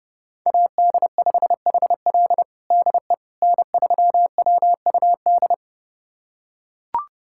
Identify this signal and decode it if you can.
Morse code, heard as a keyed tone, about 32 wpm, 700 Hz. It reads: AD5HL DE N3WUD
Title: USA Callsign DE Callsign - 5 Characters - 30wpm